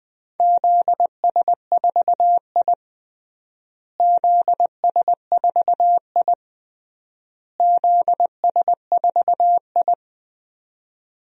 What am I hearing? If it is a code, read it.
Morse code, heard as a keyed tone, about 20 wpm, 700 Hz. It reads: ZS4I ZS4I ZS4I
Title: DXCC-100 Call Signs - 4 Characters - Repeated 3x - 20wpm